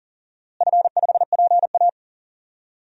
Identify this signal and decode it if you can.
Morse code, heard as a keyed tone, about 40 wpm, 700 Hz. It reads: F5PA
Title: DXCC-100 Call Signs - 4 Characters - Repeated 2x - 40wpm